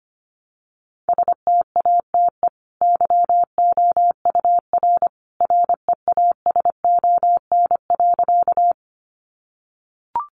Morse code, heard as a keyed tone, about 25 wpm, 700 Hz. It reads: STATE YOUR REASON.
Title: Sentences from Top 700 Words 25wpm